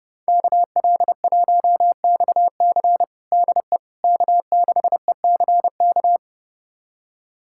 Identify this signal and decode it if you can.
Morse code, heard as a keyed tone, about 30 wpm, 700 Hz. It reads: KL1XC DE K6ECK
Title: USA Callsign DE Callsign - 5 Characters - 30wpm